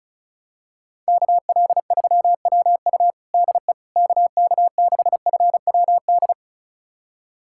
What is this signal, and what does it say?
Morse code, heard as a keyed tone, about 34 wpm, 700 Hz. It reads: KL3WU DE KK6FWD
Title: USA Callsign DE Callsign 35wpm